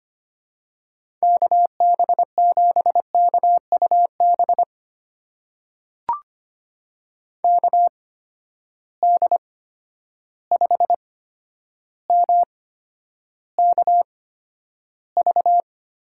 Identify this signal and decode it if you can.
Morse code, heard as a keyed tone, about 25 wpm, spaced as for 11 wpm, 700 Hz. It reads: KB7KUB KD5MKV
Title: USA Call Signs - 6 Characters - 8x Character Spacing - 25wpm